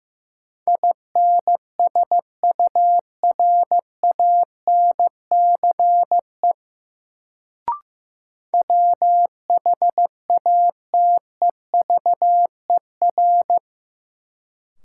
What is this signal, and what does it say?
Morse code, heard as a keyed tone, about 15 wpm, 700 Hz. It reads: INSURANCE WHATEVER
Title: Sentences from Top 1100 Words - New Word Drill - 15wpm